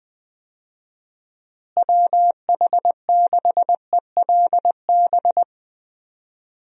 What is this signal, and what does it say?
Morse code, heard as a keyed tone, about 20 wpm, 700 Hz. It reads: WH6ELB
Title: USA Call Signs - 6 Characters - Repeated 3x - 20wpm